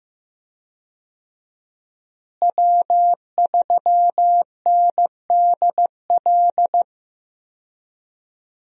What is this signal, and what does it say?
Morse code, heard as a keyed tone, about 15 wpm, 700 Hz. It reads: W3NDL
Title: USA Call Signs - 5 Characters - Repeated 3x - 15wpm